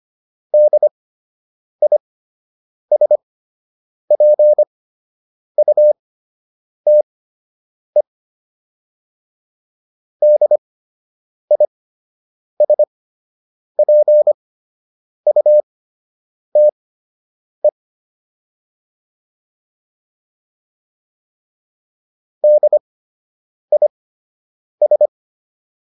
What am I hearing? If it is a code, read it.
Morse code, heard as a keyed tone, about 25 wpm, spaced as for 8 wpm, 600 Hz. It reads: DISPUTE DISPUTE DIS